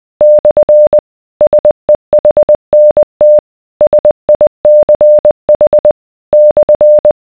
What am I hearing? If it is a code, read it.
Morse code, heard as a keyed tone, about 20 wpm, 600 Hz. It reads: / SEHNT SICH /